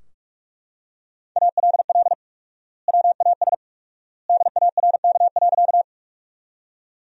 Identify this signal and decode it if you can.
Morse code, heard as a keyed tone, about 42 wpm, 700 Hz. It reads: ALL WAS DARK.
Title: 4x Word-Spacing - Sentences from Top 500 Words - 45wpm